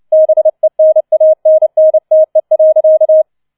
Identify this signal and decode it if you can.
Morse code, heard as a keyed tone, about 29 wpm, 630 Hz. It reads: BENANNTE.